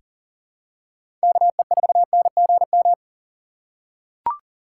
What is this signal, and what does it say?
Morse code, heard as a keyed tone, about 40 wpm, 700 Hz. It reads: KE4NGM
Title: USA Call Signs - 6 Characters - 3x Character Spacing - 40wpm